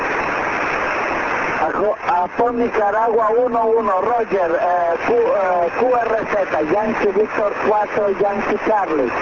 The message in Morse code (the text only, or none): none